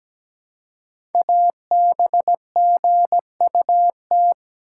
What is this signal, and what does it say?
Morse code, heard as a keyed tone, about 17 wpm, 700 Hz. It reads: ABGUT
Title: Code Groups - 5 Letters - 17wpm